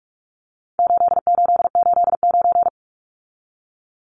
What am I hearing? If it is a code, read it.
Morse code, heard as a keyed tone, about 44 wpm, 700 Hz. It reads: 8889
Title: Numbers - 4-Digit 45wpm